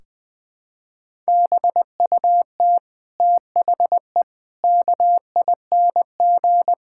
Morse code, heard as a keyed tone, about 21 wpm, 700 Hz. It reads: BUT THE KING
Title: Sentences from Top 1600 Words 20wpm